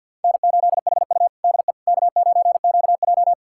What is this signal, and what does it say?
Morse code, heard as a keyed tone, about 50 wpm, 690 Hz. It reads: N8HU DE K9QJ
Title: USA Callsign DE Callsign - 4 Characters - 50wpm